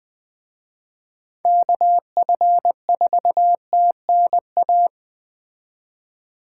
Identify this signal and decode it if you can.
Morse code, heard as a keyed tone, about 20 wpm, 700 Hz. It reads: KF4TNA